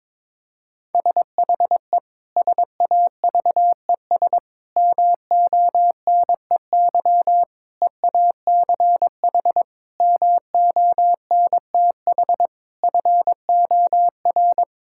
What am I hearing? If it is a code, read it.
Morse code, heard as a keyed tone, about 22 wpm, 700 Hz. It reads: SHE SAVES MONEY EACH MONTH FOR
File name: Difficult Sentences with Hints - 22wpm